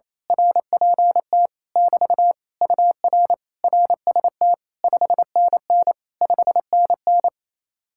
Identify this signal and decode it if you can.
Morse code, heard as a keyed tone, about 27 wpm, 700 Hz. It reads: RPT <BT> UR RST 5NN 5NN
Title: Speed-Racing Common QSO Phrases - 28wpm